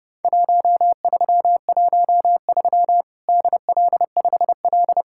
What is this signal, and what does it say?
Morse code, heard as a keyed tone, about 30 wpm, 700 Hz. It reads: 1313 DL5L
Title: Worked All Europe DX Contest - QTC Exchange - 30wpm